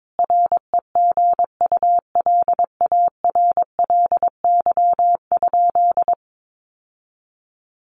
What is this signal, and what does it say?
Morse code, heard as a keyed tone, about 22 wpm, 700 Hz. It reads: REGULARLY?
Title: Sentences from Top 1400 Words 22wpm